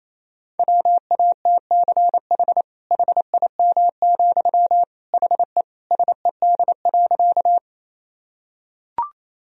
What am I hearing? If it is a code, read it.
Morse code, heard as a keyed tone, about 28 wpm, 700 Hz. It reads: WATCH HIM, HE SED.
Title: Sentences from Top 300 Words and Abbreviations 28wpm